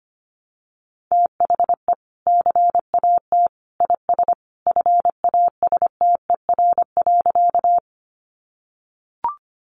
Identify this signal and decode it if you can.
Morse code, heard as a keyed tone, about 25 wpm, 700 Hz. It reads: THE CAT IS FASTER.
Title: Additional Easy Phrases 25wpm